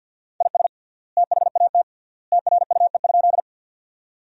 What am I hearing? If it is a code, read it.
Morse code, heard as a keyed tone, about 50 wpm, 700 Hz. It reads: IS THAT TRUE?
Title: 3x Word-Spacing - Sentences from Top 400 Words - 50wpm